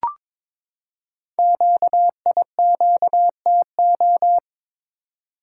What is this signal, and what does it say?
Morse code, heard as a keyed tone, about 22 wpm, 700 Hz. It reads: QIQTO
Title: Code Groups - 5 Letters - 22wpm